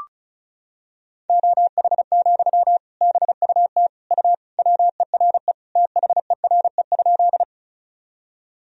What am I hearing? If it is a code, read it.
Morse code, heard as a keyed tone, about 34 wpm, 700 Hz. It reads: OH, BUT U WERE THERE?
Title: Sentences from Top 500 Words and Abbreviations 35wpm